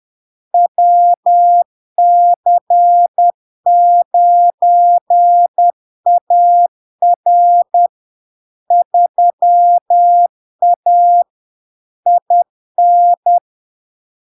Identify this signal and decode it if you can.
Morse code, heard as a keyed tone, about 10 wpm, 700 Hz. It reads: WC9AR 3A IN
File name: LARK-FD-2022-Log-in-CW-10wpm-700-Hz